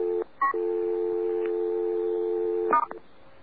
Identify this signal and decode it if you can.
Dial tone (North America), 350 + 440 Hz.